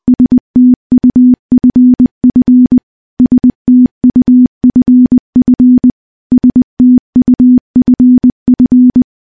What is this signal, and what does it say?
Morse code, heard as a keyed tone, about 20 wpm, 250 Hz. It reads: STUFF STUFF STUFF
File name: morse / morse-training-20WPM